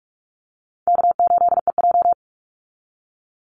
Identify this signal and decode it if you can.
Morse code, heard as a keyed tone, about 44 wpm, 700 Hz. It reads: K8EJ
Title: Call Signs Repeated 3x - 45wpm